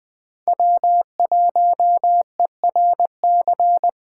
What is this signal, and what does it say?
Morse code, heard as a keyed tone, about 21 wpm, 700 Hz. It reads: W1ERC